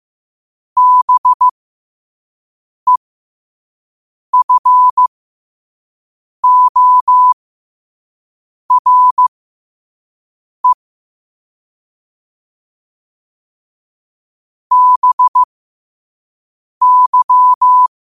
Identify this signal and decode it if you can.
Morse code, heard as a keyed tone, about 15 wpm, spaced as for 5 wpm, 1000 Hz. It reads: BEFORE BY
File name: qotdfiles / 5wpm